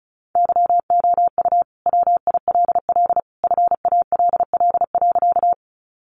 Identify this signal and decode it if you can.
Morse code, heard as a keyed tone, about 34 wpm, 700 Hz. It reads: YOU WILL FALL.
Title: Sentences from Top 500 Words 35wpm